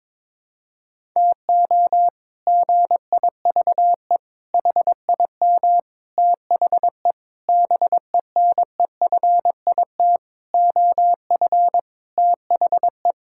Morse code, heard as a keyed tone, about 22 wpm, 700 Hz. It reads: TO GIVE HIM THE BENEFIT OF THE
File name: practice / 22wpm / Sets of 8 Words - 22wpm